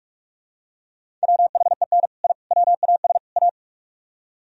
Morse code, heard as a keyed tone, about 47 wpm, 700 Hz. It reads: WHEN I WAS A